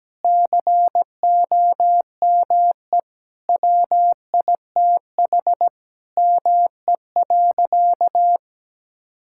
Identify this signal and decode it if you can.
Morse code, heard as a keyed tone, about 17 wpm, 700 Hz. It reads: COME WITH ME.
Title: Sentences from Top 1000 Words - Review Part 6 - 17wpm